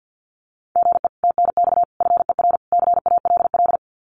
Morse code, heard as a keyed tone, about 50 wpm, 700 Hz. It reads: GE TNX FER CALL
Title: Common QSO Phrases Encore 50wpm